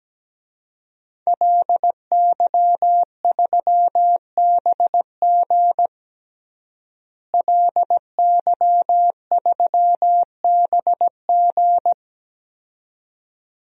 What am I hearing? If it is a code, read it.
Morse code, heard as a keyed tone, about 17 wpm, 700 Hz. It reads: LY3BG LY3BG
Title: DXCC-100 Call Signs - 5 Characters - Repeated 2x - 17wpm